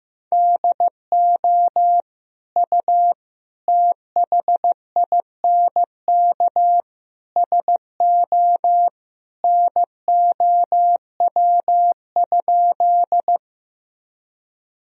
Morse code, heard as a keyed tone, about 15 wpm, 700 Hz. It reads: DO U THINK SO NOW?
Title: Sentences from Top 200 Words and Abbreviations 15wpm